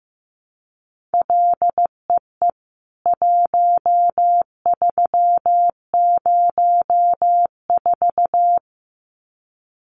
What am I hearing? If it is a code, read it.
Morse code, heard as a keyed tone, about 14 wpm, 700 Hz. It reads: LEE 1304
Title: CWT Exchange 15wpm